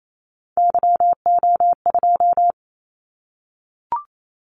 Morse code, heard as a keyed tone, about 28 wpm, 700 Hz. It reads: YO2
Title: Callsign Prefixes for Top 25 DXCC - 28wpm